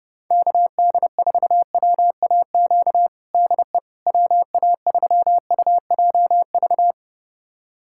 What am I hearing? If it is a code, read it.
Morse code, heard as a keyed tone, about 32 wpm, 700 Hz. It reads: KD4WAQ DE WA3UJV
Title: USA Callsign DE Callsign 30wpm